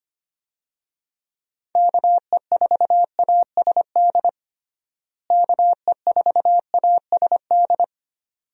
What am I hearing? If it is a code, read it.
Morse code, heard as a keyed tone, about 25 wpm, 700 Hz. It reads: KE4ASD KE4ASD
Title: USA Call Signs - 6 Characters - Repeated 2x - 25wpm